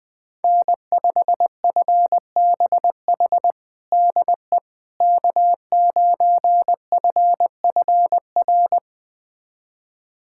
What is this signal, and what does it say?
Morse code, heard as a keyed tone, about 20 wpm, 700 Hz. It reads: N5FBH DE K9FFR
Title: USA Callsign DE Callsign - 5 Characters - 20wpm